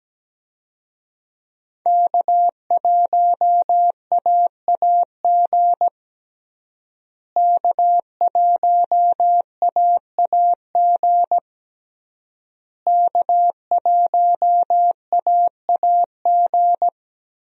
Morse code, heard as a keyed tone, about 17 wpm, 700 Hz. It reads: K1AAG K1AAG K1AAG
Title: USA Call Signs - 5 Characters - Repeated 3x - 17wpm